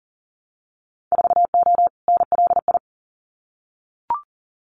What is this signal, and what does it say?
Morse code, heard as a keyed tone, about 40 wpm, 700 Hz. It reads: 4O NLI
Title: Winter Field Day Exchanges Encore 40wpm